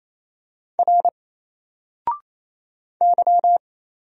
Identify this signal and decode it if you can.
Morse code, heard as a keyed tone, about 27 wpm, 700 Hz. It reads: R Y
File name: Single-Letter-Number-28wpm